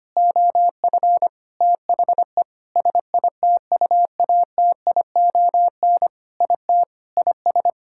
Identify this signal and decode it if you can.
Morse code, heard as a keyed tone, about 25 wpm, 700 Hz. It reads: OF THE SITUATION IT IS
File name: practice / 25wpm / Sentences with 9-Word Phrases - 25wpm